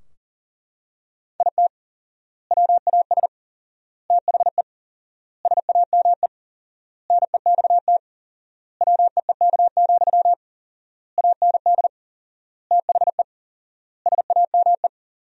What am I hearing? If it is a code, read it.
Morse code, heard as a keyed tone, about 37 wpm, 700 Hz. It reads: IT WAS THE SAME NEXT WEEK, AND THE SAME
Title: 4x Word-Spacing - Sentences from Top 500 Words - 40wpm